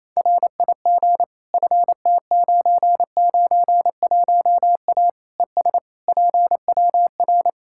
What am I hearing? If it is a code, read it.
Morse code, heard as a keyed tone, about 28 wpm, 700 Hz. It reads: RIG FT991A ES PWR
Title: Practice QSO - Exchange 2 - Info - 28wpm